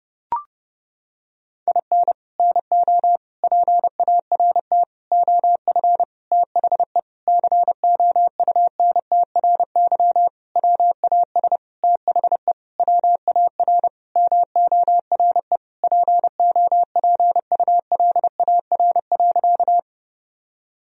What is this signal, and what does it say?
Morse code, heard as a keyed tone, about 30 wpm, 700 Hz. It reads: IN NO PART OF THE COUNTRY WAS THE WAR MORE POPULAR.
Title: Sentences from Top 1000 Words - Review Part 3 - 30wpm